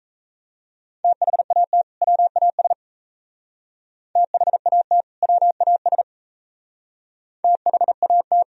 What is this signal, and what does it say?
Morse code, heard as a keyed tone, about 35 wpm, 700 Hz. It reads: THAT WAS THAT WAS THAT
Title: Speed-Racing - Sets of 2 Words 28wpm